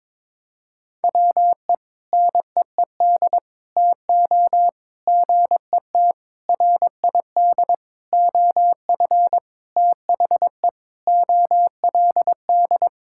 Morse code, heard as a keyed tone, about 22 wpm, 700 Hz. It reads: WE NEED TO GET RID OF THE OLD
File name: practice / 22wpm / Sentences with 4-Word Phrases - 22wpm